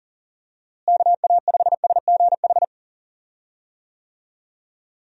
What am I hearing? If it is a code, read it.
Morse code, heard as a keyed tone, about 40 wpm, 700 Hz. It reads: KA5SGH
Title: USA Call Signs - 6 Characters - Repeated 2x - 40wpm